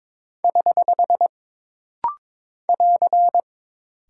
Morse code, heard as a keyed tone, about 22 wpm, 700 Hz. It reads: <HH> <AR>
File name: practice / 22wpm / Single Punctuation-Prosign - Rapid-Fire - 22wpm